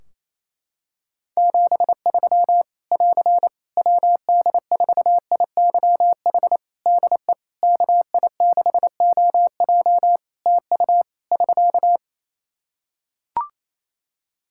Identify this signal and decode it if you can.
Morse code, heard as a keyed tone, about 28 wpm, 700 Hz. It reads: 73 <AR> WD4IYH DE KI6OJ TU <SK>
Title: K1USN SST QSO Practice 28wpm